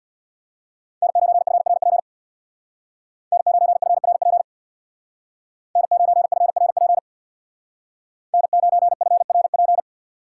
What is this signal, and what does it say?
Morse code, heard as a keyed tone, about 64 wpm, 700 Hz. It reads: N9FRP N9FRP N9FRP N9FRP